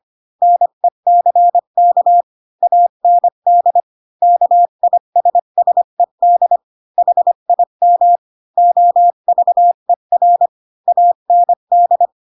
Morse code, heard as a keyed tone, about 25 wpm, 700 Hz. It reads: NECK AND KISSED HIM OVER AND
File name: Pinocchio / Pinocchio ch8 20 25 ik1hge